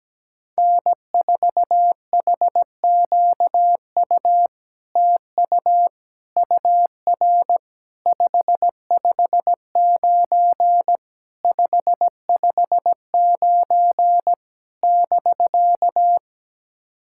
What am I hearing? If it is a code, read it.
Morse code, heard as a keyed tone, about 17 wpm, 700 Hz. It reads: N4HQU TU UR 559 559 <BK>
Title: SOTA QSO Practice 17wpm